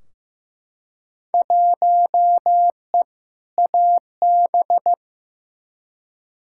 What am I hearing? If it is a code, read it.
Morse code, heard as a keyed tone, about 15 wpm, 700 Hz. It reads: 1E AB
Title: Field Day Exchanges - 15wpm